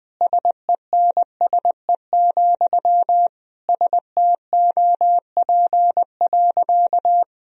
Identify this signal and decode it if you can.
Morse code, heard as a keyed tone, about 21 wpm, 700 Hz. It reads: SENSE, STOP.
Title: Sentences from Top 1000 Words - Review Part 4 - 20wpm